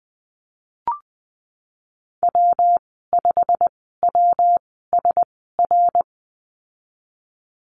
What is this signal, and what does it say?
Morse code, heard as a keyed tone, about 20 wpm, spaced as for 14 wpm, 700 Hz. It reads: W5WSR
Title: 128 Lesson 21g - Number 6 - Call Signs 20wpm